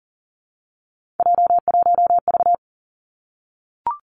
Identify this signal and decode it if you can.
Morse code, heard as a keyed tone, about 41 wpm, 700 Hz. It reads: J1V
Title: Three Challenging Letter-Number Combinations 40wpm